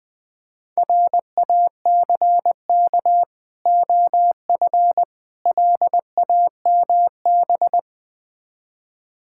Morse code, heard as a keyed tone, about 20 wpm, 690 Hz. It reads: RACK OF LAMB
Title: Difficult Words with Hints - 20wpm